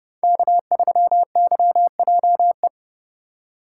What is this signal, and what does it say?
Morse code, heard as a keyed tone, about 30 wpm, 700 Hz. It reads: K3YJE
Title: USA Call Signs - 5 Characters - 4x Character Spacing - 30wpm